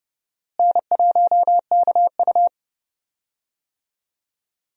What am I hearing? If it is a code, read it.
Morse code, heard as a keyed tone, about 31 wpm, 700 Hz. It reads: N1KU
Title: USA Call Signs - 4 Characters - Repeated 2x - 30wpm